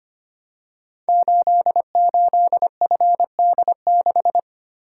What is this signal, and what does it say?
Morse code, heard as a keyed tone, about 25 wpm, 700 Hz. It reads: 88FD6
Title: Code Groups - 5 Letter:Numbers - 25wpm